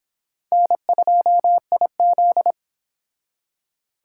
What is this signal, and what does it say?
Morse code, heard as a keyed tone, about 30 wpm, 700 Hz. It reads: N2IZ